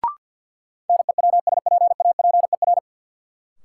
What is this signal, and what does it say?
Morse code, heard as a keyed tone, about 50 wpm, 700 Hz. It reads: NEWSPAPER
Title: Sentences from Top 1100 Words - New Word Drill Alt - 50wpm